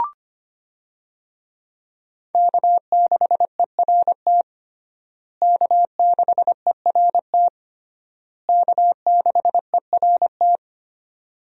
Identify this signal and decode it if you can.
Morse code, heard as a keyed tone, about 25 wpm, 700 Hz. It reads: K6ERT K6ERT K6ERT
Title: USA Call Signs - 5 Characters - Repeated 3x - 25wpm